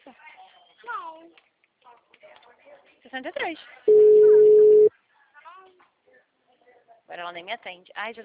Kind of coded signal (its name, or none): telephone call progress tone